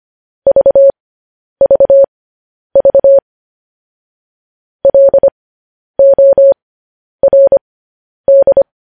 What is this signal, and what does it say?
Morse code, heard as a keyed tone, about 25 wpm, spaced as for 10 wpm, 550 Hz. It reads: VVV LORD
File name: podcast / News headlines-1x-25x10